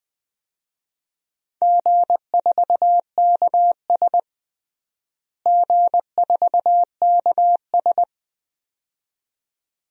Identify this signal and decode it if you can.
Morse code, heard as a keyed tone, about 21 wpm, 700 Hz. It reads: G4KS G4KS
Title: DXCC-100 Call Signs - 4 Characters - Repeated 2x - 20wpm